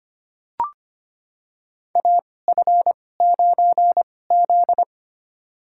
Morse code, heard as a keyed tone, about 25 wpm, spaced as for 18 wpm, 700 Hz. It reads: AF9Z
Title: USA Call Signs - 4 Characters - 2x Character Spacing - 25wpm